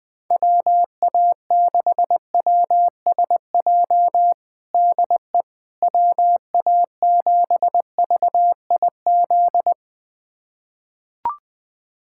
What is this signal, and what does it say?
Morse code, heard as a keyed tone, about 20 wpm, 700 Hz. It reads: WA6WSJ DE WA7VIZ
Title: USA Callsign DE Callsign - 6 Characters - 20wpm